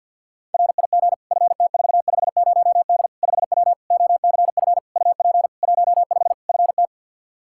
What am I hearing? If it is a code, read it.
Morse code, heard as a keyed tone, about 52 wpm, 700 Hz. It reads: RIG FT450D 5W OCF UP 15 FT
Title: K1USN SST QSO Practice 50wpm